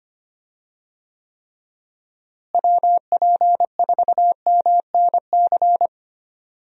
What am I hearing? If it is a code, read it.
Morse code, heard as a keyed tone, about 25 wpm, 700 Hz. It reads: WP4MNC
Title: USA Call Signs - 6 Characters - Repeated 2x - 25wpm